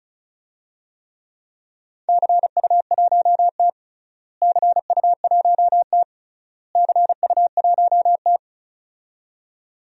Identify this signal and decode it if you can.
Morse code, heard as a keyed tone, about 36 wpm, 700 Hz. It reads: CU1T CU1T CU1T
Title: DXCC-100 Call Signs - 4 Characters - Repeated 3x - 35wpm